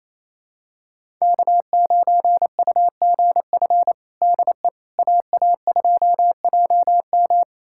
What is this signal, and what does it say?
Morse code, heard as a keyed tone, about 28 wpm, 700 Hz. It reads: K9UGF DE AA2JM
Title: USA Callsign DE Callsign - 5 Characters - 28wpm